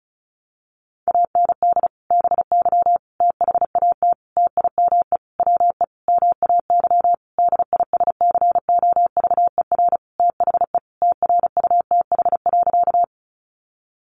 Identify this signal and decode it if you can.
Morse code, heard as a keyed tone, about 34 wpm, 700 Hz. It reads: AND BY THAT TIME WE MAY DISCOVER THE TRUTH.
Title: Sentences from Top 1100 Words 35wpm